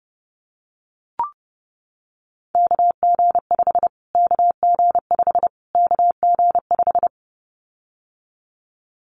Morse code, heard as a keyed tone, about 30 wpm, 700 Hz. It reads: KG5 KG5 KG5
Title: Callsign Prefixes for Top 25 DXCC - Repeated 3x - 30wpm